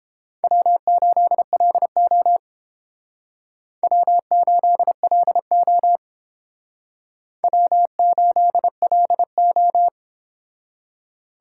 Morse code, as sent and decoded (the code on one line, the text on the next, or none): .-- ---.. .-.. --- / .-- ---.. .-.. --- / .-- ---.. .-.. ---
W8LO W8LO W8LO